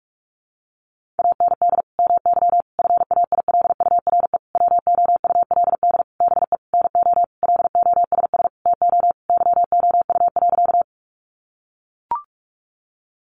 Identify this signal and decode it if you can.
Morse code, heard as a keyed tone, about 44 wpm, 700 Hz. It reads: AND MY FAILURE WOULD BE NO LOSS TO YOU.
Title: Sentences from Top 1400 Words 45wpm